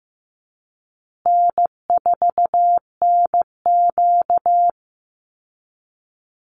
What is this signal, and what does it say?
Morse code, heard as a keyed tone, about 15 wpm, 700 Hz. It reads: N4NQ
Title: Easy Call Signs Repeated 2x - 15wpm